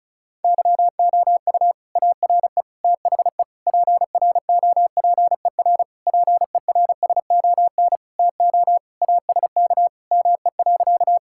Morse code, heard as a keyed tone, about 36 wpm, 700 Hz. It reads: YOU ARE THE PROPER PERSON TO ASK ME.
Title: Sentences from Top 1000 Words - Review Part 5 - 35wpm